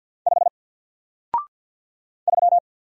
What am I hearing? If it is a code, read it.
Morse code, heard as a keyed tone, about 50 wpm, 700 Hz. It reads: 5 3
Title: Single Letter-Number 50wpm